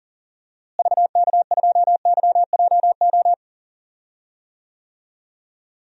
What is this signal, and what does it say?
Morse code, heard as a keyed tone, about 40 wpm, 700 Hz. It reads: VK2YJO
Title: DXCC-100 Call Signs - 6 Characters - Repeated 2x - 40wpm